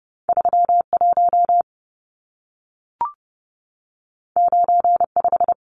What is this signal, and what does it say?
Morse code, heard as a keyed tone, about 30 wpm, 700 Hz. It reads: 31 95
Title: Numbers - 2-Digit 30wpm